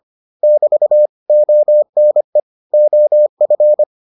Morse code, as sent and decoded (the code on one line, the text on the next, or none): -...- / --- -. / --- ..-.
<BT> ONE OF